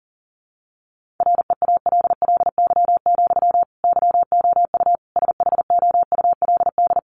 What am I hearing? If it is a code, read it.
Morse code, heard as a keyed tone, about 40 wpm, 700 Hz. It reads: REALLY, YOU SHOULD
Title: Sentences from Top 1400 Words 40wpm